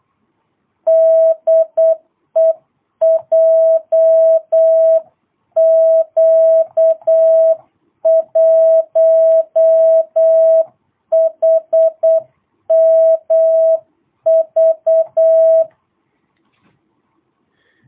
Morse code, heard as a keyed tone, about 8 wpm, 660 Hz. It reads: DE JQ1HMV
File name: jh8cbh / 2022cw